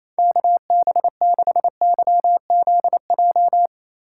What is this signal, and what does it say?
Morse code, heard as a keyed tone, about 35 wpm, 700 Hz. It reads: KB6YZJ